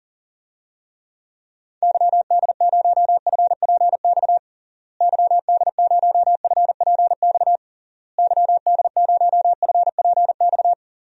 Morse code, heard as a keyed tone, about 42 wpm, 700 Hz. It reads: YD0FPX YD0FPX YD0FPX
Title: DXCC-100 Call Signs - 6 Characters - Repeated 3x - 40wpm